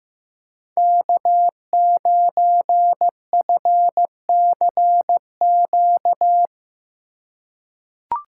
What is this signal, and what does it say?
Morse code, heard as a keyed tone, about 15 wpm, 700 Hz. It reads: K9FCQ
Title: USA Call Signs - 5 Characters - 2x Character Spacing - 15wpm